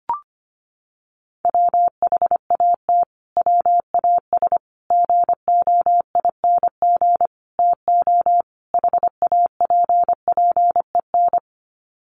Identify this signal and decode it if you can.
Morse code, heard as a keyed tone, about 25 wpm, 700 Hz. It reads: WHAT WAS GOING TO HAPPEN
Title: Sets of 5 Words Encore 25wpm